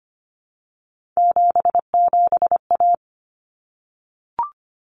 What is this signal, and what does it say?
Morse code, heard as a keyed tone, about 25 wpm, 700 Hz. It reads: 77A
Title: Common Cut Numbers - 3-Digit 25wpm